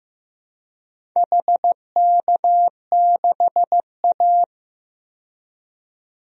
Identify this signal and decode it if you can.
Morse code, heard as a keyed tone, about 15 wpm, 700 Hz. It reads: HK6A
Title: DXCC-100 Call Signs - 4 Characters - 15wpm